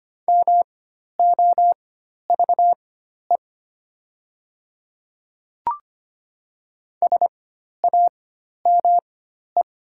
Morse code, heard as a keyed tone, about 25 wpm, spaced as for 12 wpm, 700 Hz. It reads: MOVE SAME
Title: 4x Character Spacing - Top 200 Words - 25wpm